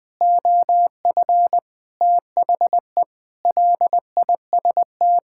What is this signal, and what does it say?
Morse code, heard as a keyed tone, about 19 wpm, 700 Hz. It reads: OF THE LIST
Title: Sentences with 6-Word Phrases - 20wpm